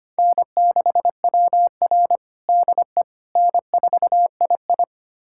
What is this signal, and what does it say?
Morse code, heard as a keyed tone, about 25 wpm, 700 Hz. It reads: N6WR DE N4II
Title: USA Callsign DE Callsign - 4 Characters - 25wpm